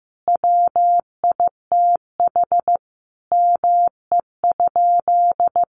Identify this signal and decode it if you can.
Morse code, heard as a keyed tone, about 15 wpm, 700 Hz. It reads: WITH ME?
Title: Sentences from Top 1100 Words 15wpm